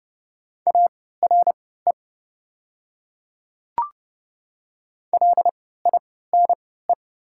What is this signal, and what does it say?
Morse code, heard as a keyed tone, about 30 wpm, spaced as for 17 wpm, 700 Hz. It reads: ARE LINE
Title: 3x Character Spacing - Top 200 Words - 30wpm